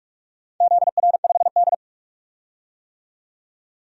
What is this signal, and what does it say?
Morse code, heard as a keyed tone, about 46 wpm, 700 Hz. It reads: ZR5D
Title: DXCC-100 Call Signs - 4 Characters - 45wpm